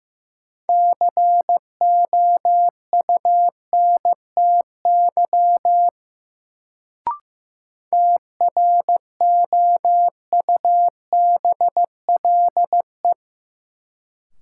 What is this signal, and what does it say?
Morse code, heard as a keyed tone, about 15 wpm, 700 Hz. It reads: COUNTY TROUBLE
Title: Sentences from Top 1200 Words - New Word Drill Alt - 15wpm